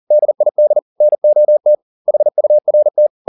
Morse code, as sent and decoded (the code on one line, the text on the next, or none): -.. .. -.. / -. --- - / .... ..- .-. -
DID NOT HURT